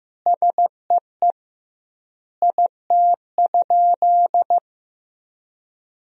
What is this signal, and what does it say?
Morse code, heard as a keyed tone, about 15 wpm, 700 Hz. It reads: SEE IT?
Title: 2x Word-Spacing - Sentences from Top 100 Words - 15wpm